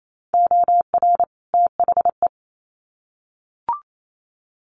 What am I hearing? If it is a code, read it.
Morse code, heard as a keyed tone, about 28 wpm, 700 Hz. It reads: OR THE
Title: Sets of 2 Words 28wpm